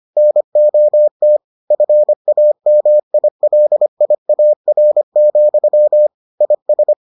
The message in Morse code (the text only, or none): NOT FAMILIAR, IS